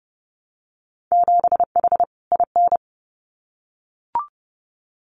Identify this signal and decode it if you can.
Morse code, heard as a keyed tone, about 30 wpm, 700 Hz. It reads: 7H IN
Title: Winter Field Day Exchanges Encore 30wpm